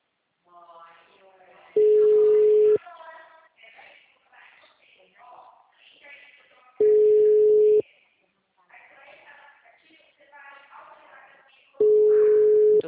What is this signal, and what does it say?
Ringback tone (Europe, 425 Hz), with speech.
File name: out- -2041- - - .30163